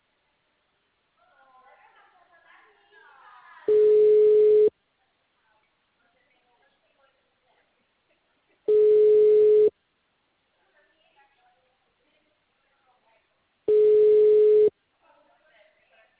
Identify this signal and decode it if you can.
Ringback tone (Europe, 425 Hz), with speech.